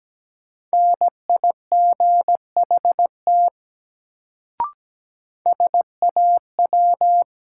Morse code, heard as a keyed tone, about 17 wpm, 700 Hz. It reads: NIGHT SAW
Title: Sentences from Top 300 Words - New Word Drill - 17wpm